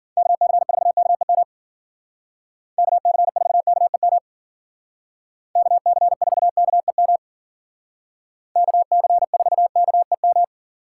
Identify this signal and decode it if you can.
Morse code, heard as a keyed tone, about 50 wpm, 700 Hz. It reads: KC4KEM KC4KEM KC4KEM KC4KEM